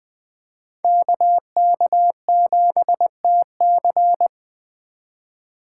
Morse code, heard as a keyed tone, about 20 wpm, 700 Hz. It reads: KK7TC